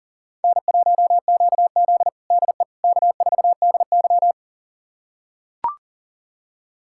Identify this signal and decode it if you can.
Morse code, heard as a keyed tone, about 40 wpm, 700 Hz. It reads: N1QZ DE K4DY
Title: USA Callsign DE Callsign - 4 Characters - 40wpm